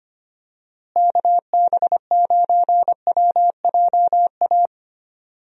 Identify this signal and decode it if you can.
Morse code, heard as a keyed tone, about 25 wpm, 700 Hz. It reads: KB9WJA
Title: USA Call Signs - 6 Characters - Repeated 3x - 25wpm